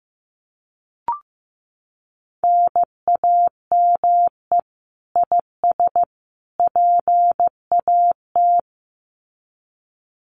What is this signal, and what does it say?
Morse code, heard as a keyed tone, about 15 wpm, 700 Hz. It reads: NAME IS PAT
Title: Common QSO Phrases 15wpm